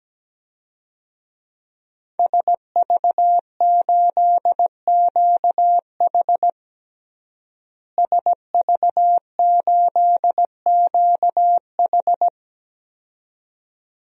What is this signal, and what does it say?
Morse code, heard as a keyed tone, about 17 wpm, 700 Hz. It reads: SV8QH SV8QH
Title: DXCC-100 Call Signs - 5 Characters - Repeated 2x - 17wpm